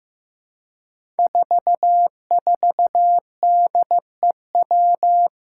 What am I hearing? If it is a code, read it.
Morse code, heard as a keyed tone, about 15 wpm, 700 Hz. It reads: 44DEW
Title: Five Challenging Letter-Number Combinations 15wpm